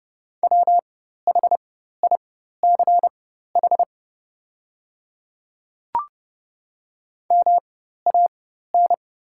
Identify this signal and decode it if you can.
Morse code, heard as a keyed tone, about 30 wpm, spaced as for 14 wpm, 700 Hz. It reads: WHICH MAN
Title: 4x Character Spacing - Top 500 Words - 30wpm